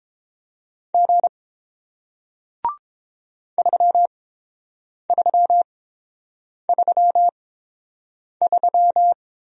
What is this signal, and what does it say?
Morse code, heard as a keyed tone, about 30 wpm, spaced as for 7 wpm, 700 Hz. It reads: G 3333